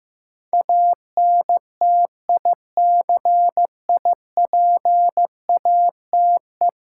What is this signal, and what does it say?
Morse code, heard as a keyed tone, about 15 wpm, 700 Hz. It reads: ANTICIPATE
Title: Sentences from Top 1500 Words - New Word Drill - 15wpm